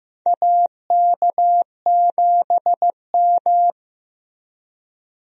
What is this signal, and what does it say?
Morse code, heard as a keyed tone, about 15 wpm, 700 Hz. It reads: AK7M
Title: USA Call Signs - 4 Characters - Repeated 2x - 15wpm